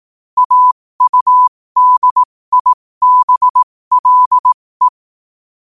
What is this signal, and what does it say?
Morse code, heard as a keyed tone, about 18 wpm, spaced as for 15 wpm, 1000 Hz. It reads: AUDIBLE
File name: qotdfiles / 15wpm